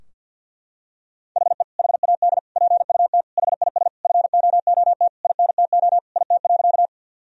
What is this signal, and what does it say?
Morse code, heard as a keyed tone, about 50 wpm, 700 Hz. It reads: HE HAD PUT HIS FOOT INTO IT.
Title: Sentences from Top 1000 Words - Review Part 3 - 50wpm